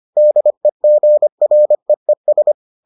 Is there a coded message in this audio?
Yes: Morse code from 0.2 to 2.5 s, heard as a keyed tone, about 25 wpm, 600 Hz. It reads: DEGREES